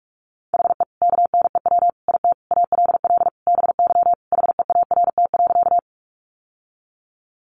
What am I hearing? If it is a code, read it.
Morse code, heard as a keyed tone, about 44 wpm, 700 Hz. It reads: HE KNEW IT ALL BY HEART.
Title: Sentences from Top 1200 Words 45wpm